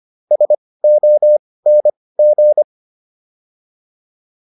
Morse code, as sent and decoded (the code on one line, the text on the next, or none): ... --- -. --.
SONG